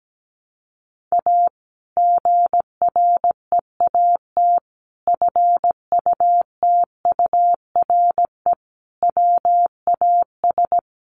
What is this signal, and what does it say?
Morse code, heard as a keyed tone, about 17 wpm, 700 Hz. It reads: A GREAT FUTURE WAS
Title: Sentences from Top 700 Words 17wpm